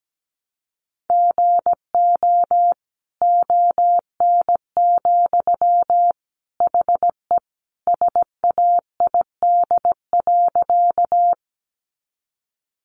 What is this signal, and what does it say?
Morse code, heard as a keyed tone, about 17 wpm, 700 Hz. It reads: GO ON, HE SAID.
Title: Sentences from Top 100 Words 17wpm